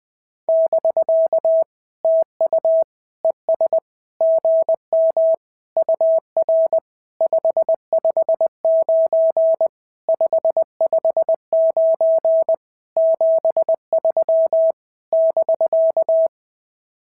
Morse code, heard as a keyed tone, about 19 wpm, 650 Hz. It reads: <BK> TU ES GM UR 559 559 73 <BK>
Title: SOTA QSO Practice 20wpm